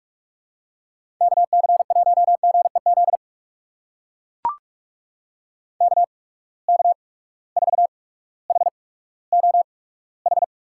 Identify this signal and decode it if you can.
Morse code, heard as a keyed tone, about 43 wpm, spaced as for 19 wpm, 700 Hz. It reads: KC1GEZ KK4HOH
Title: USA Call Signs - 6 Characters - 8x Character Spacing - 45wpm